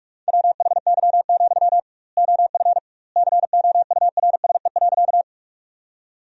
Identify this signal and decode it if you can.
Morse code, heard as a keyed tone, about 46 wpm, 700 Hz. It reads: WHY, OF COURSE.
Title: 2x Word-Spacing - Sentences from Top 500 Words - 45wpm